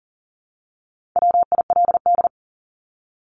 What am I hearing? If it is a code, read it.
Morse code, heard as a keyed tone, about 37 wpm, 700 Hz. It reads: WILD
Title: Top 1501-1750 Words 40wpm